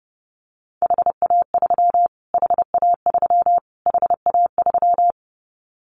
Morse code, heard as a keyed tone, about 30 wpm, 690 Hz. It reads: HA3 HA3 HA3
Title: Callsign Prefixes for Top 25 DXCC - Repeated 3x - 30wpm